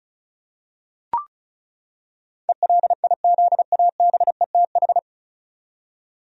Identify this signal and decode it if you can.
Morse code, heard as a keyed tone, about 36 wpm, 700 Hz. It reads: ELIZABETH
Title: USA Common Names - 35wpm